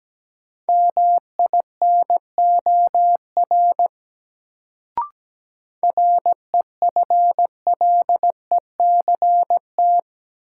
Morse code, heard as a keyed tone, about 17 wpm, 700 Hz. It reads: MINOR REFLECT
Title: Sentences from Top 1500 Words - New Word Drill Alt - 17wpm